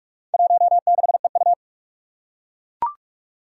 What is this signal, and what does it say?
Morse code, heard as a keyed tone, about 41 wpm, 700 Hz. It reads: 16EU
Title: Four Challenging Letter-Number Combinations 45wpm